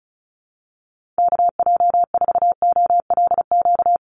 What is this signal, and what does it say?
Morse code, heard as a keyed tone, about 34 wpm, 700 Hz. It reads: KJ4OLQ
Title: USA Call Signs Repeated 2x 35wpm